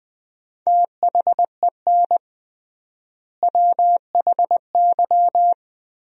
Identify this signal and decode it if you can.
Morse code, heard as a keyed tone, about 20 wpm, 700 Hz. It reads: THEN WHY
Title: 3x Word-Spacing - Sentences from Top 200 Words - 20wpm